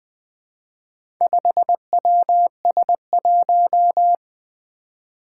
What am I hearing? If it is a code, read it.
Morse code, heard as a keyed tone, about 20 wpm, 700 Hz. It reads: 5WS1
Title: Four Challenging Letter-Number Combinations 20wpm